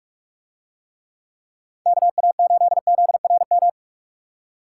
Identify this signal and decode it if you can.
Morse code, heard as a keyed tone, about 44 wpm, 700 Hz. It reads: KA8ZRM